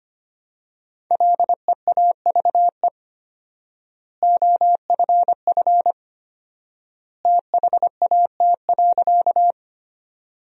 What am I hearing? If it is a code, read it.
Morse code, heard as a keyed tone, about 25 wpm, 700 Hz. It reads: LEAVE OFF THAT.
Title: 4x Word-Spacing - Sentences from Top 400 Words - 25wpm